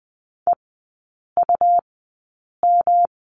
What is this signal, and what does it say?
Morse code, heard as a keyed tone, about 20 wpm, spaced as for 8 wpm, 690 Hz. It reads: EUM
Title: 191 Lesson 31e - Letter v - Three Characters 20wpm